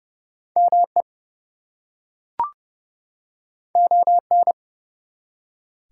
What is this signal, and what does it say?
Morse code, heard as a keyed tone, about 31 wpm, 700 Hz. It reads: ME ON
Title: Two Letter Words - 30wpm